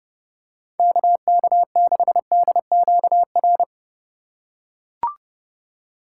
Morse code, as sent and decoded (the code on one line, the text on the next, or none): -.- -.- -.... -.. --.- .-.
KK6DQR